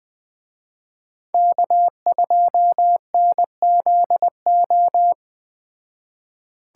Morse code, heard as a keyed tone, about 20 wpm, 700 Hz. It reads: K2NZO
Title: Speed Racing - USA Call Signs 17wpm